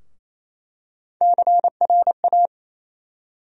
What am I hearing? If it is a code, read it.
Morse code, heard as a keyed tone, about 28 wpm, 700 Hz. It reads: CRA
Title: USA Call Sign Postfixes - 3 Letters 28wpm